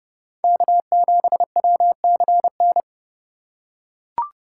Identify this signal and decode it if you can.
Morse code, heard as a keyed tone, about 28 wpm, 700 Hz. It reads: K7WCN